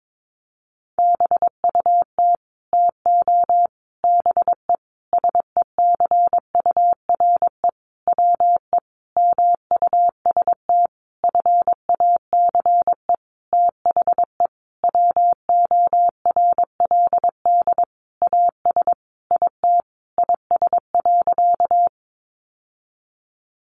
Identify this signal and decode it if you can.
Morse code, heard as a keyed tone, about 22 wpm, 700 Hz. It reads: BUT TO BE SECURE WE MUST FACE THE WORLD AS IT IS.
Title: Sentences from Top 1200 Words 22wpm